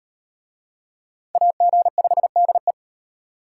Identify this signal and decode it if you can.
Morse code, heard as a keyed tone, about 38 wpm, 700 Hz. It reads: AG5DE